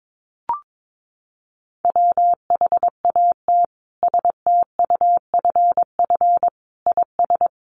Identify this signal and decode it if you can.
Morse code, heard as a keyed tone, about 22 wpm, 700 Hz. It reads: WHAT STUFF IS
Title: Sentences from Top 1400 Words 22wpm